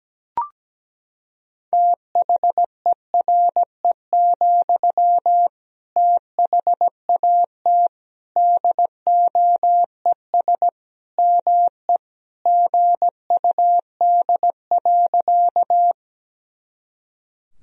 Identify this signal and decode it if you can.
Morse code, heard as a keyed tone, about 17 wpm, 700 Hz. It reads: THERE, THAT DOES ME GUD.
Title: Sentences from Top 200 Words and Abbreviations 17wpm